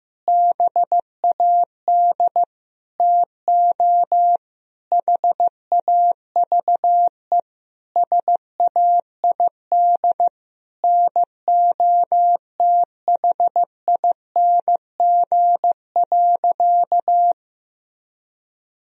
Morse code, heard as a keyed tone, about 16 wpm, 700 Hz. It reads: BAD TO HAVE SAID NOTHING.
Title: Sentences from Top 1000 Words - Review Part 4 - 15wpm